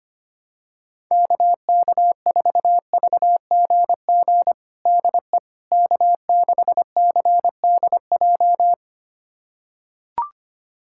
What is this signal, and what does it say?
Morse code, heard as a keyed tone, about 25 wpm, 700 Hz. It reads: KK4VGG DE K6CDJ
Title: USA Callsign DE Callsign 25wpm